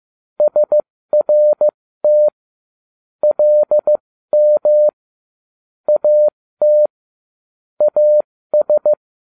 Morse code, heard as a keyed tone, about 15 wpm, spaced as for 13 wpm, 600 Hz. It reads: SRT LM AT AS